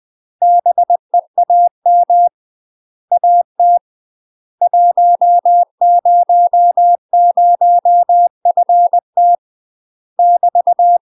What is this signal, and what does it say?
Morse code, heard as a keyed tone, about 20 wpm, 700 Hz. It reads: BEAM AT 100FT <BT>
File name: UK MASTER QSOs TRAINING / 20WPM / UK MASTER QSOs 0028